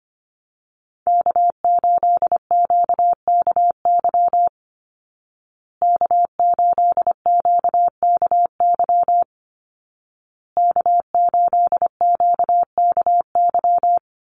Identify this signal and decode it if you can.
Morse code, heard as a keyed tone, about 25 wpm, 700 Hz. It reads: K8QKY K8QKY K8QKY
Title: Call Signs Repeated 2x - 25wpm